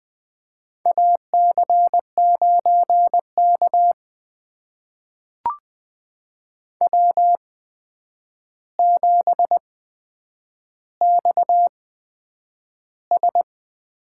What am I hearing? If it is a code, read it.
Morse code, heard as a keyed tone, about 20 wpm, spaced as for 9 wpm, 700 Hz. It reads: AC9K W7XS
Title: USA Call Signs - 4 Characters - 8x Character Spacing - 20wpm